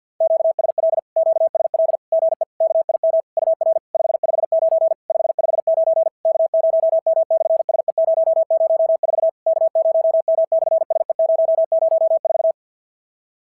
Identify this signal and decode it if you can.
Morse code, heard as a keyed tone, about 50 wpm, 690 Hz. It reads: QSL QSL GE KIM UR 559 559 K0M/SE004 K0M/SE004
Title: SOTA QSO Practice 50wpm